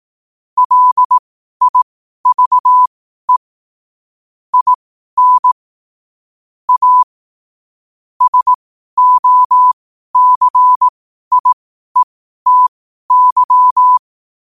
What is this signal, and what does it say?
Morse code, heard as a keyed tone, about 18 wpm, spaced as for 12 wpm, 1000 Hz. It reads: LIVE IN A SOCIETY
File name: qotdfiles / 12wpm